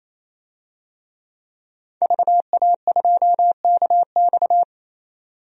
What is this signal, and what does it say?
Morse code, heard as a keyed tone, about 28 wpm, 700 Hz. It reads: VA2KX
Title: DXCC-100 Call Signs - 5 Characters - 28wpm